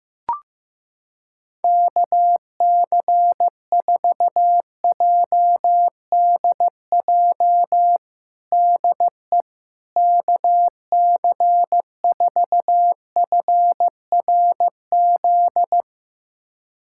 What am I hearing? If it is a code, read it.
Morse code, heard as a keyed tone, about 16 wpm, 700 Hz. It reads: KC4JDJ DE KC4FRZ
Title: USA Callsign DE Callsign - 6 Characters - 15wpm